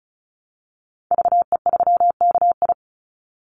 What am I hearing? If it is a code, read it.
Morse code, heard as a keyed tone, about 34 wpm, 700 Hz. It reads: VE3KI
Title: Call Signs - 35wpm